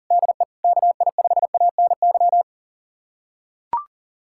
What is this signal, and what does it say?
Morse code, heard as a keyed tone, about 37 wpm, 700 Hz. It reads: DE KI5ANY
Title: USA Callsign DE Callsign - 6 Characters - 40wpm